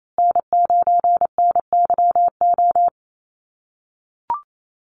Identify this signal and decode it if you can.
Morse code, heard as a keyed tone, about 28 wpm, 700 Hz. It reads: N9NYO
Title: USA Call Signs Repeated 2x 28wpm